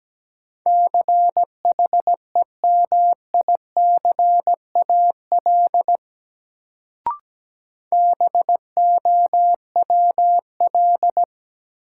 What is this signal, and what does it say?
Morse code, heard as a keyed tone, about 17 wpm, 700 Hz. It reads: CHEMICAL BOWL
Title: Sentences from Top 1200 Words - New Word Drill - 17wpm